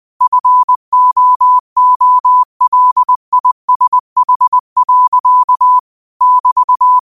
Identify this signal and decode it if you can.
Morse code, heard as a keyed tone, about 20 wpm, 1000 Hz. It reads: FOOLISH. <BT>
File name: qotdfiles / 20wpm